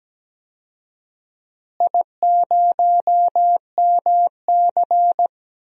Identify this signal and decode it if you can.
Morse code, heard as a keyed tone, about 17 wpm, 700 Hz. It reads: I0MC
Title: DXCC-100 Call Signs - 4 Characters - Repeated 3x - 17wpm